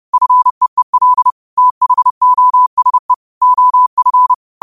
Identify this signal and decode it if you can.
Morse code, heard as a keyed tone, about 30 wpm, 1000 Hz. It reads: FEEL THOSE OF
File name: qotdfiles / 30wpm